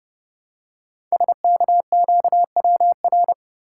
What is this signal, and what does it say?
Morse code, heard as a keyed tone, about 30 wpm, 700 Hz. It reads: SKQWR
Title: Code Groups - 5 Letter:Numbers - 30wpm